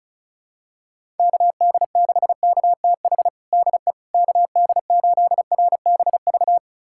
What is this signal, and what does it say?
Morse code, heard as a keyed tone, about 36 wpm, 700 Hz. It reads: KD6KTH DE KD8RBV
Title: USA Callsign DE Callsign - 6 Characters - 35wpm